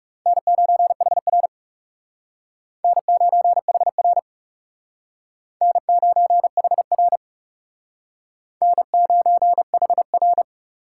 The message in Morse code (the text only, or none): N9HR N9HR N9HR N9HR